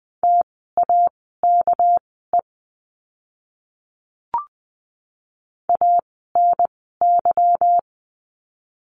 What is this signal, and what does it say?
Morse code, heard as a keyed tone, about 20 wpm, spaced as for 15 wpm, 700 Hz. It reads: TAKE ANY
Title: 244 Lesson 39f - Letter z - Words 20wpm